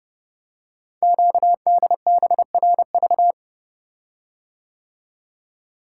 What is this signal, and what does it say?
Morse code, heard as a keyed tone, about 30 wpm, 690 Hz. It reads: QDBRV
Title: Code Groups - 5 Letters - 30wpm